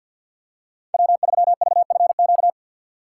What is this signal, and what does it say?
Morse code, heard as a keyed tone, about 53 wpm, 700 Hz. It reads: W3VFQ
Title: USA Call Signs - 5 Characters - 3x Character Spacing - 50wpm